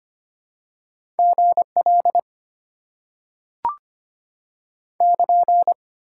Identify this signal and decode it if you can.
Morse code, heard as a keyed tone, about 25 wpm, 700 Hz. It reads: GL <KN>
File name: Top 90 QSO Elements - 8x Character Spacing - 25wpm